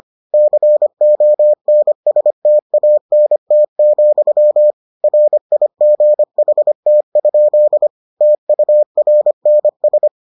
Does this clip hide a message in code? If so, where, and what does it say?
Yes: Morse code from 0.3 to 10.1 s, heard as a keyed tone, about 25 wpm, 600 Hz. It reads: CONSTANT, RIGHT? TURNS